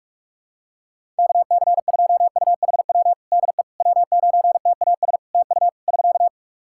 Morse code, heard as a keyed tone, about 46 wpm, 700 Hz. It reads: KC2UHW DE W9TAS TU <SK>
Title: Practice QSO - Exchange 4 - Ending - 45wpm